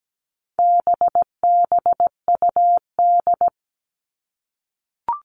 Morse code, heard as a keyed tone, about 17 wpm, 700 Hz. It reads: BBUD
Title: All Cut Numbers - 4-Digit 17wpm